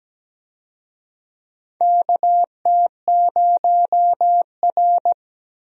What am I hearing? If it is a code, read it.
Morse code, heard as a keyed tone, about 17 wpm, 700 Hz. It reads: KT0R
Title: USA Call Signs - 4 Characters - Repeated 3x - 17wpm